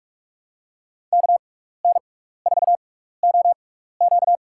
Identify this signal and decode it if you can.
Morse code, heard as a keyed tone, about 44 wpm, spaced as for 15 wpm, 700 Hz. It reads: KN4OQ
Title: USA Call Signs - 5 Characters - 6x Character Spacing - 45wpm